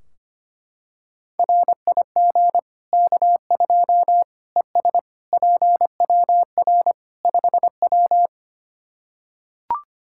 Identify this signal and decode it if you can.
Morse code, heard as a keyed tone, about 25 wpm, 700 Hz. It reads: RIG K2 ES PWR 5W
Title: Practice QSO - Exchange 2 - Info - 25wpm